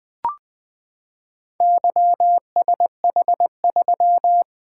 Morse code, heard as a keyed tone, about 20 wpm, 700 Hz. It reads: YSH3
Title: Four Letter-Number Combinations - 20wpm